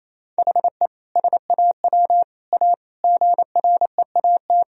Morse code, heard as a keyed tone, about 28 wpm, 700 Hz. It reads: HE SAW A GREAT
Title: Sentences from Top 1000 Words - Review Part 1 - 28wpm